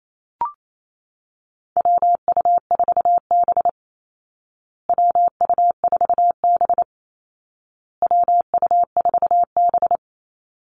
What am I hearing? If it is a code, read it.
Morse code, heard as a keyed tone, about 28 wpm, 700 Hz. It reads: WU4B WU4B WU4B
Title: Call Signs Repeated 2x - 28wpm